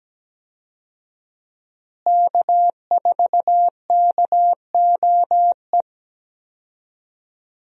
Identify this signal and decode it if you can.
Morse code, heard as a keyed tone, about 17 wpm, 700 Hz. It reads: K4KOE
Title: USA Call Signs - 5 Characters - Repeated 3x - 17wpm